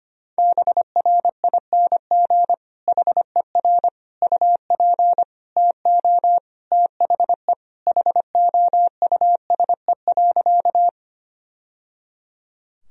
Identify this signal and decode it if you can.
Morse code, heard as a keyed tone, about 25 wpm, 700 Hz. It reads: BRING HER UP TO THE HOUSE.
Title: Sentences from Top 1000 Words - Review Part 3 - 25wpm